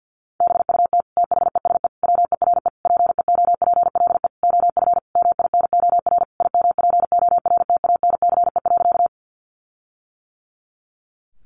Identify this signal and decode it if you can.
Morse code, heard as a keyed tone, about 50 wpm, 700 Hz. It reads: BUT THESE WERE PEOPLE OF MINOR IMPORTANCE.
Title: Sentences from Top 1500 Words 50wpm